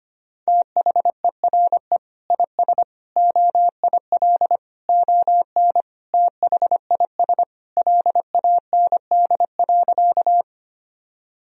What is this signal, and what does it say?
Morse code, heard as a keyed tone, about 25 wpm, 700 Hz. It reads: THERE IS OIL ON THIS LAND.
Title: Sentences from Top 1000 Words - Review Part 4 - 25wpm